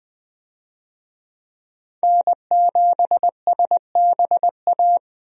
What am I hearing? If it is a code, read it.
Morse code, heard as a keyed tone, about 20 wpm, 700 Hz. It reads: N7SBA
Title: USA Call Signs - 5 Characters - Repeated 3x - 20wpm